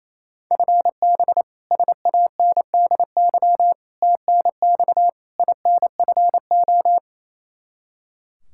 Morse code, heard as a keyed tone, about 28 wpm, 700 Hz. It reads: FB SANDY TNX INFO
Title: Practice QSO - Exchange 3 - More Info - 28wpm